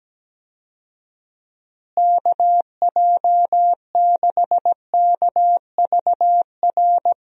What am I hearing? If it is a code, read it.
Morse code, heard as a keyed tone, about 17 wpm, 700 Hz. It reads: KJ6KVR
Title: USA Call Signs - 6 Characters - Repeated 2x - 17wpm